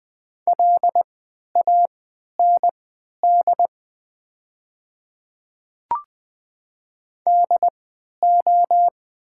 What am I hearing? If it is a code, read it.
Morse code, heard as a keyed tone, about 20 wpm, spaced as for 11 wpm, 700 Hz. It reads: LAND DO
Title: 3x Character Spacing - Top 200 Words - 20wpm